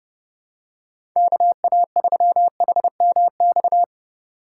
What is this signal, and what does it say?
Morse code, heard as a keyed tone, about 30 wpm, 700 Hz. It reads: KA3HMX
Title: USA Call Signs - 6 Characters - 3x Character Spacing - 30wpm